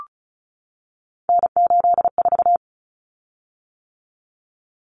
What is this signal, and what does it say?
Morse code, heard as a keyed tone, about 34 wpm, 700 Hz. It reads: N84
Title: Common Cut Numbers - 3-Digit 35wpm